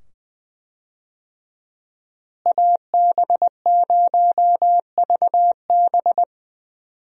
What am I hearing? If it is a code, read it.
Morse code, heard as a keyed tone, about 21 wpm, 700 Hz. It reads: AB0VB
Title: USA Call Signs - 5 Characters - Repeated 2x - 20wpm